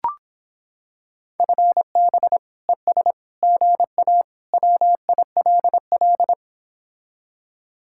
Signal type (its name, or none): Morse code